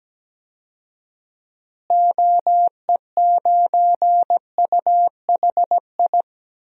Morse code, heard as a keyed tone, about 17 wpm, 700 Hz. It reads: OE9UHI
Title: DXCC-100 Call Signs - 6 Characters - Repeated 2x - 17wpm